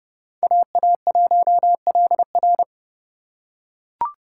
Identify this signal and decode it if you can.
Morse code, heard as a keyed tone, about 27 wpm, 700 Hz. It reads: AA1LR